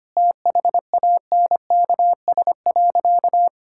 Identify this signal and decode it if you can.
Morse code, heard as a keyed tone, about 25 wpm, 700 Hz. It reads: THANKS.
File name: Sentences with 2-Word Phrases - 25wpm